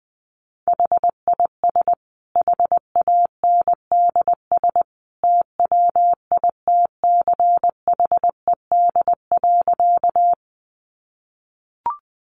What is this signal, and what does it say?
Morse code, heard as a keyed tone, about 19 wpm, 700 Hz. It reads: HIS HANDS TWITCHED.
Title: A Sound Of Thunder 20wpm